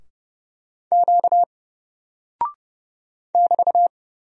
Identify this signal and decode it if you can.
Morse code, heard as a keyed tone, about 30 wpm, 700 Hz. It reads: Q <BT>
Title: Single Character (Letter:Number:Punctuation:Prosign) 30wpm